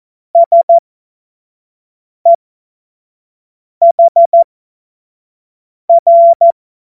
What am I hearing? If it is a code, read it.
Morse code, heard as a keyed tone, about 14 wpm, spaced as for 5 wpm, 680 Hz. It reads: SEHR
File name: Morsekurs / Witze / Witz 19. Oktober 2021, 25 F70